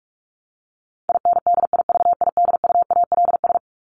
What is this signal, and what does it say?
Morse code, heard as a keyed tone, about 44 wpm, 700 Hz. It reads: INDIVIDUALS
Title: ES:S Suffix - Top 100 Words with Base Words 45wpm